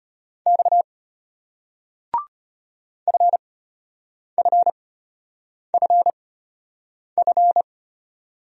Morse code, heard as a keyed tone, about 33 wpm, spaced as for 8 wpm, 700 Hz. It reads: X FFFF